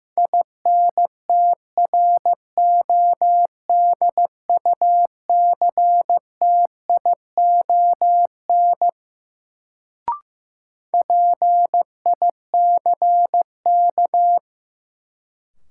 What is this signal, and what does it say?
Morse code, heard as a keyed tone, about 15 wpm, 700 Hz. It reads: INTRODUCTION PICK
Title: Sentences from Top 1000 Words - New Word Drill - 15wpm